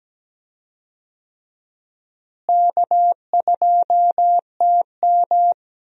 Morse code, heard as a keyed tone, about 17 wpm, 700 Hz. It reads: K2TM
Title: USA Call Signs - 4 Characters - Repeated 2x - 17wpm